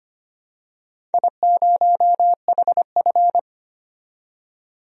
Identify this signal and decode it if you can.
Morse code, heard as a keyed tone, about 25 wpm, 700 Hz. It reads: I0HF
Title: DXCC-100 Call Signs - 4 Characters - 25wpm